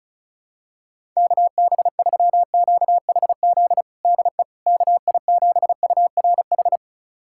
Morse code, heard as a keyed tone, about 36 wpm, 700 Hz. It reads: KB3QHZ DE KI7URH
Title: USA Callsign DE Callsign - 6 Characters - 35wpm